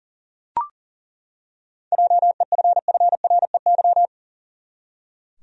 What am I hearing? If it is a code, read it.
Morse code, heard as a keyed tone, about 40 wpm, 700 Hz. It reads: JEFFREY
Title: USA Common Names - 40wpm